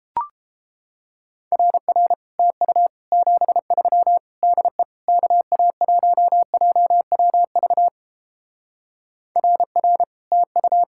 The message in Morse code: RR TU 73 DE KA1JWV RR TU

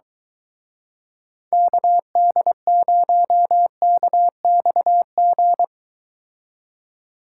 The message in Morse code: KD0KXG

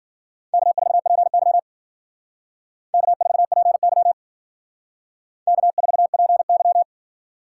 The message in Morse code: K4PY K4PY K4PY

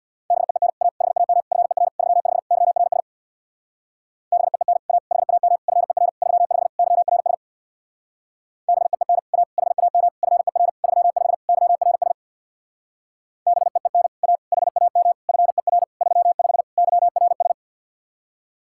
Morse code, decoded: BEEN A HAM FER 35 YRS BEEN A HAM FER 35 YRS BEEN A HAM FER 35 YRS BEEN A HAM FER 35 YRS

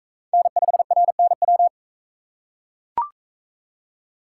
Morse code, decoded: N5RNW

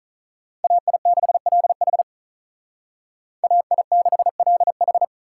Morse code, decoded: AI6LH AI6LH

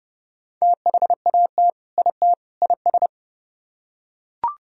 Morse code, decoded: THAT IT IS

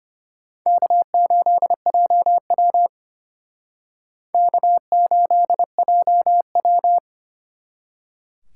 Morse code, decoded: K8JW K8JW

